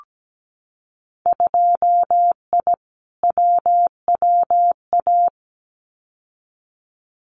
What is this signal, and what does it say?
Morse code, heard as a keyed tone, about 17 wpm, 700 Hz. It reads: 2I WWA
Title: Winter Field Day Exchanges Encore 17wpm